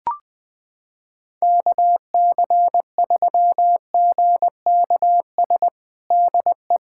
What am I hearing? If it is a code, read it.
Morse code, heard as a keyed tone, about 21 wpm, 700 Hz. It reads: KC3GKS DE
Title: USA Callsign DE Callsign - 6 Characters - 20wpm